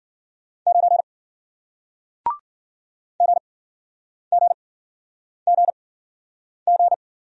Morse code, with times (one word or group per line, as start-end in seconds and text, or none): 0.7-1.0 9
3.2-6.9 GGGG